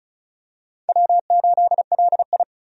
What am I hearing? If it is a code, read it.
Morse code, heard as a keyed tone, about 36 wpm, 700 Hz. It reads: W8LI
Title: USA Call Signs - 4 Characters - 35wpm